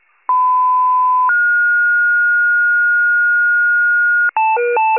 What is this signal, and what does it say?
Two-tone page, heard as a one-second tone, then a three-second tone, about 1010 then 1430 Hz. It is heard from 0.3 to 4.3 s.
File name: pager / 94 FIRE / wcalert